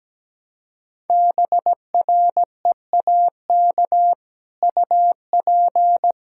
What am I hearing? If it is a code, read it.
Morse code, heard as a keyed tone, about 17 wpm, 700 Hz. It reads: BREAK UP
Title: Sentences from Top 1600 Words 17wpm